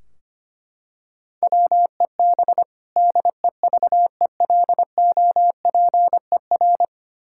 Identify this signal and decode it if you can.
Morse code, heard as a keyed tone, about 25 wpm, 700 Hz. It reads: WEB DEVELOPER
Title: Practice QSO - Exchange 3 - More Info - 25wpm